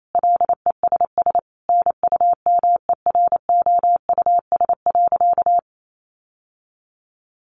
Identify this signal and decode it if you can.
Morse code, heard as a keyed tone, about 28 wpm, 700 Hz. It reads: LESS NUMEROUS.
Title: Sentences from Top 1400 Words 28wpm